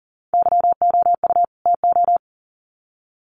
Morse code, heard as a keyed tone, about 40 wpm, 700 Hz. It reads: YOU TO
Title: Sets of 2 Words 40wpm